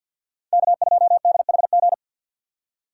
Morse code, heard as a keyed tone, about 48 wpm, 700 Hz. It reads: K2DHG
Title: USA Call Signs - 5 Characters - 6x Character Spacing - 50wpm